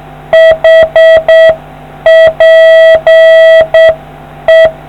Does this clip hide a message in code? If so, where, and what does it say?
Yes: Morse code from 0.3 to 4.7 s, heard as a keyed tone, about 7 wpm, 650 Hz. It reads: HPE